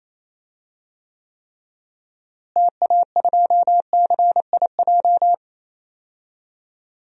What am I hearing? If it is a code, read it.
Morse code, heard as a keyed tone, about 28 wpm, 700 Hz. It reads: TA2CIJ
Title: DXCC-100 Call Signs - 6 Characters - Repeated 3x - 28wpm